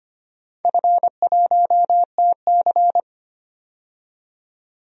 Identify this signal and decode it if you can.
Morse code, heard as a keyed tone, about 25 wpm, 700 Hz. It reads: F1TC
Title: DXCC-100 Call Signs - 4 Characters - 25wpm